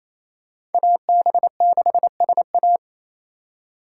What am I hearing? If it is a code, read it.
Morse code, heard as a keyed tone, about 28 wpm, 700 Hz. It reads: AB6SA
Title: USA Call Signs - 5 Characters - 4x Character Spacing - 28wpm